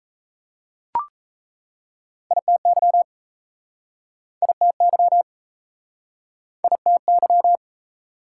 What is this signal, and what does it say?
Morse code, heard as a keyed tone, about 39 wpm, 690 Hz. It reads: ITY ITY ITY